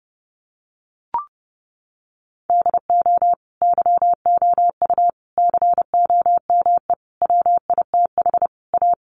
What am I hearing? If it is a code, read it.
Morse code, heard as a keyed tone, about 30 wpm, 700 Hz. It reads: DO YOU COME WITH A
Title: Sentences from Top 1200 Words 30wpm